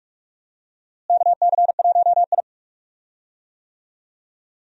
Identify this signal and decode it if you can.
Morse code, heard as a keyed tone, about 46 wpm, 700 Hz. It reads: KC1I
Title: USA Call Signs - 4 Characters - Repeated 3x - 45wpm